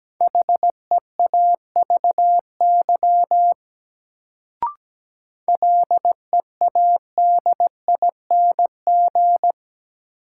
Morse code, heard as a keyed tone, about 17 wpm, 700 Hz. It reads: HEAVY LEADING
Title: Sentences from Top 1100 Words - New Word Drill Alt - 17wpm